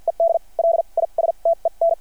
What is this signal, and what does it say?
Morse code, heard as a keyed tone, about 52 wpm, spaced as for 38 wpm, 660 Hz. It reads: ED LISTEN
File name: archived / CRSnet